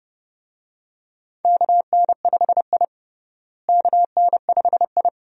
Morse code, heard as a keyed tone, about 30 wpm, 700 Hz. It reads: KN5I KN5I
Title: USA Call Signs - 4 Characters - Repeated 2x - 30wpm